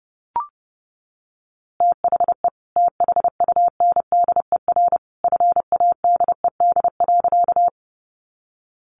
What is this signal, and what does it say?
Morse code, heard as a keyed tone, about 30 wpm, 700 Hz. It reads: THE THUNDER FADED.
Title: A Sound Of Thunder 30wpm